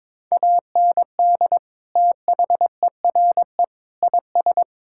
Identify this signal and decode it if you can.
Morse code, heard as a keyed tone, about 22 wpm, 700 Hz. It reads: AND THERE IS
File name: practice / 22wpm / Sentences with 8-Word Phrases - 22wpm